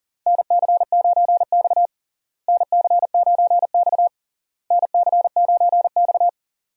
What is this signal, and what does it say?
Morse code, heard as a keyed tone, about 40 wpm, 700 Hz. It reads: NC9X NC9X NC9X
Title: USA Call Signs - 4 Characters - Repeated 3x - 40wpm